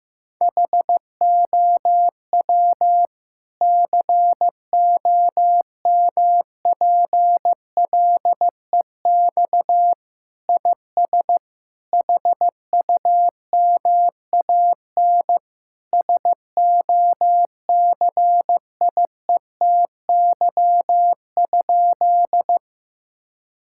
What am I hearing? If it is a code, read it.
Morse code, heard as a keyed tone, about 15 wpm, 700 Hz. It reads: HOW COMPLEX IS HUMAN SOCIETY?
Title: Sentences from Top 1000 Words - Review Part 6 - 15wpm